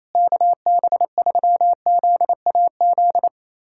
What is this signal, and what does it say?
Morse code, heard as a keyed tone, about 27 wpm, 700 Hz. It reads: KB3ZAZ
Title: USA Callsign DE Callsign - 6 Characters - 28wpm